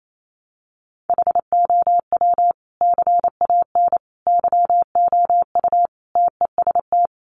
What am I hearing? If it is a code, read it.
Morse code, heard as a keyed tone, about 28 wpm, 700 Hz. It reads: HOW CAN YOU TEST
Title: Sentences from Top 1100 Words 28wpm